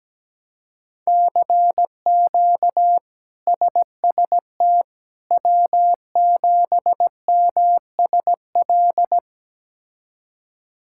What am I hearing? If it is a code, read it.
Morse code, heard as a keyed tone, about 17 wpm, 700 Hz. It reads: CQ SST W7MSL
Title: K1USN SST QSO Practice - 17wpm